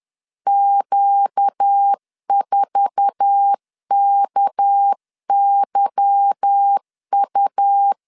Morse code, heard as a keyed tone, about 10 wpm, 800 Hz. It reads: Q4KYU
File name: LEZIONI / 9 28 10wpm 2842